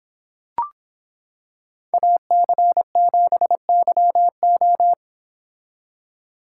Morse code, heard as a keyed tone, about 26 wpm, 700 Hz. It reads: AC7YO